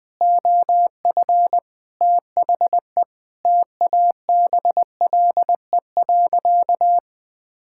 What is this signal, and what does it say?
Morse code, heard as a keyed tone, about 20 wpm, 700 Hz. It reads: OF THE TABLE.
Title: Sentences with 6-Word Phrases - 20wpm